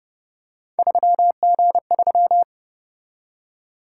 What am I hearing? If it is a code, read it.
Morse code, heard as a keyed tone, about 30 wpm, 700 Hz. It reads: 3G3